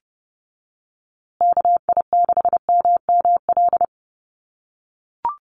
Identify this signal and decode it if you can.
Morse code, heard as a keyed tone, about 30 wpm, 700 Hz. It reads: KI6MML
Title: USA Call Signs Repeated 3x 30wpm